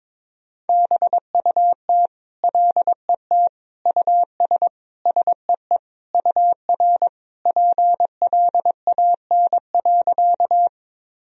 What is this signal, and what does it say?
Morse code, heard as a keyed tone, about 22 wpm, 700 Hz. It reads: BUT LET US SEE UR PLAN.
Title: Sentences from Top 500 Words and Abbreviations 22wpm